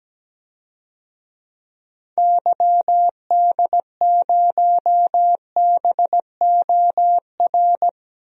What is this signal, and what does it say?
Morse code, heard as a keyed tone, about 17 wpm, 700 Hz. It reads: YD0BOR
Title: DXCC-100 Call Signs - 6 Characters - 17wpm